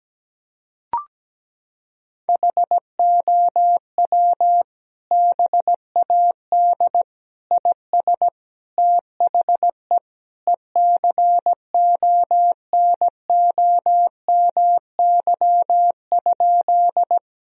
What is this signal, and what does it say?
Morse code, heard as a keyed tone, about 17 wpm, 700 Hz. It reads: HOW BAD IS THE ECONOMY?
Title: Sentences from Top 1000 Words - Review Part 3 - 17wpm